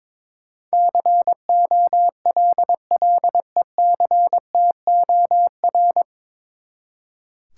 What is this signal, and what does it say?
Morse code, heard as a keyed tone, about 22 wpm, 700 Hz. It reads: COLLECTOR
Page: Morse Code Ninja practice 22wpm